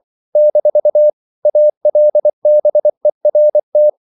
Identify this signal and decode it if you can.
Morse code, heard as a keyed tone, about 24 wpm, 600 Hz. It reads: - ALBERT
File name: CW / Practice / EnglishQuotations0224